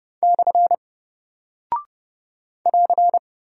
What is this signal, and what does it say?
Morse code, heard as a keyed tone, about 30 wpm, 700 Hz. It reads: / <AR>
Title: Single Character (Letter:Number:Punctuation:Prosign) Alt 30wpm